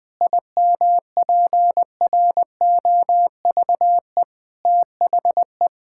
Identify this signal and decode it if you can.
Morse code, heard as a keyed tone, about 21 wpm, 700 Hz. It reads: IMPROVE THE
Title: Sentences from Top 1000 Words - Review Part 5 - 20wpm